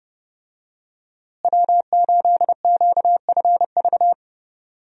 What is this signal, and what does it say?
Morse code, heard as a keyed tone, about 30 wpm, 700 Hz. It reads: W8QFV
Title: USA Call Signs - 5 Characters - Repeated 3x - 30wpm